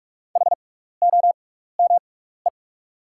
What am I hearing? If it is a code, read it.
Morse code, heard as a keyed tone, about 44 wpm, spaced as for 15 wpm, 690 Hz. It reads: HOME
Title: 6x Character Spacing - Top 500 Words - 45wpm